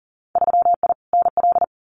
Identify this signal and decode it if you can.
Morse code, heard as a keyed tone, about 40 wpm, 700 Hz. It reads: 3I NL
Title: Winter Field Day Exchanges 40wpm